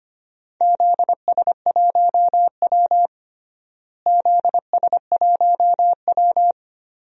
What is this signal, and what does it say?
Morse code, heard as a keyed tone, about 25 wpm, 700 Hz. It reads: ZS1W ZS1W
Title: DXCC-100 Call Signs - 4 Characters - Repeated 2x - 25wpm